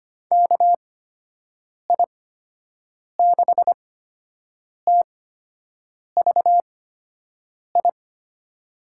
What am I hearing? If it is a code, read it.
Morse code, heard as a keyed tone, about 25 wpm, spaced as for 7 wpm, 700 Hz. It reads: KI6TVI
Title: USA Call Signs - 8x Character Spacing - 25wpm